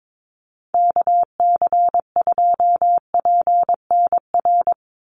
Morse code, heard as a keyed tone, about 22 wpm, 700 Hz. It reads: KC2PNR
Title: USA Call Signs 22wpm